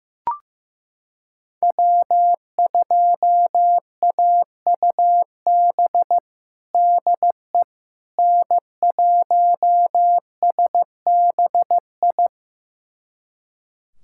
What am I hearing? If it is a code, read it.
Morse code, heard as a keyed tone, about 15 wpm, 700 Hz. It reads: W2AUB DE N1SBI
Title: USA Callsign DE Callsign 15wpm